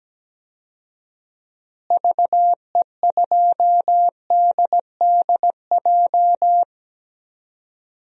Morse code, heard as a keyed tone, about 17 wpm, 700 Hz. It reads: VE2DDJ
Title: DXCC-100 Call Signs - 6 Characters - Repeated 3x - 17wpm